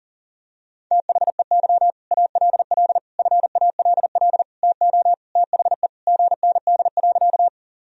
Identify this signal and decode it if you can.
Morse code, heard as a keyed tone, about 37 wpm, 700 Hz. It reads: THEY ALL FALL TO THE GND.
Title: Sentences from Top 500 Words and Abbreviations 40wpm